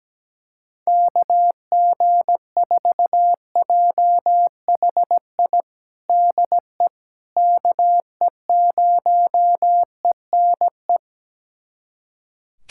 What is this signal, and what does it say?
Morse code, heard as a keyed tone, about 17 wpm, 700 Hz. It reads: KG4JHI DE KE0ENE
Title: USA Callsign DE Callsign 17wpm